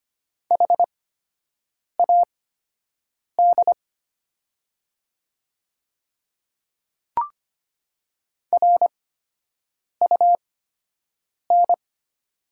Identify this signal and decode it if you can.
Morse code, heard as a keyed tone, about 25 wpm, spaced as for 7 wpm, 700 Hz. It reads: HAD RUN
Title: 8x Character Spacing - Top 500 Words - 25wpm